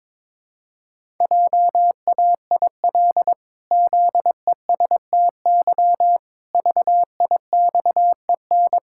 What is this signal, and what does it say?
Morse code, heard as a keyed tone, about 22 wpm, 700 Hz. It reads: JAIL ZESTY VIXEN
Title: Pangram Challenge v2 - 22wpm